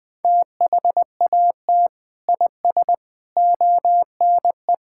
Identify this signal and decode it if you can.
Morse code, heard as a keyed tone, about 20 wpm, 700 Hz. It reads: THAT IS ONE
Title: Sentences with 5-Word Phrases - 20wpm